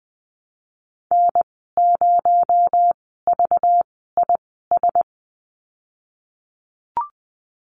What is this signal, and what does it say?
Morse code, heard as a keyed tone, about 20 wpm, spaced as for 15 wpm, 700 Hz. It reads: N0VIS
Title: 231 Lesson 37g - Letter x - Call Signs 20wpm